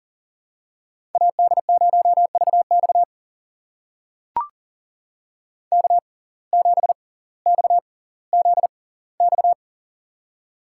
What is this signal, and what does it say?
Morse code, heard as a keyed tone, about 40 wpm, spaced as for 20 wpm, 700 Hz. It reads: AD0VX K7XZX
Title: USA Call Signs - 5 Characters - 6x Character Spacing - 40wpm